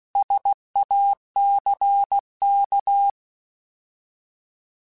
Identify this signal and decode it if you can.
Morse code, heard as a keyed tone, about 16 wpm, 800 Hz. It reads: SACK